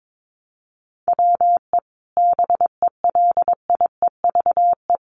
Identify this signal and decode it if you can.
Morse code, heard as a keyed tone, about 22 wpm, 700 Hz. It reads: WE BELIEVE
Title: Sentences from Top 1300 Words 22wpm